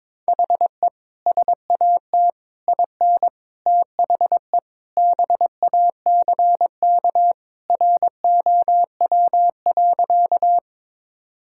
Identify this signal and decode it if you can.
Morse code, heard as a keyed tone, about 22 wpm, 700 Hz. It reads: HE SAT IN THE BACK ROW.
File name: practice / 22wpm / Sentences with 3-Word Phrases - 22wpm